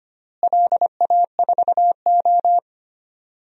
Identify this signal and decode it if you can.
Morse code, heard as a keyed tone, about 25 wpm, 700 Hz. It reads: LA4O
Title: DXCC-100 Call Signs - 4 Characters - Repeated 3x - 25wpm